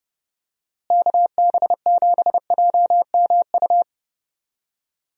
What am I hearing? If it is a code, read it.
Morse code, heard as a keyed tone, about 30 wpm, 700 Hz. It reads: KB7JMU
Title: USA Call Signs - 6 Characters - Repeated 3x - 30wpm